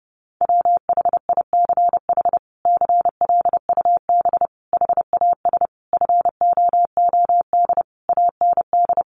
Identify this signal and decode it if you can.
Morse code, heard as a keyed tone, about 30 wpm, 700 Hz. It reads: WHICH CLUB HAS FOOD AND
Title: Sentences from Top 1200 Words 30wpm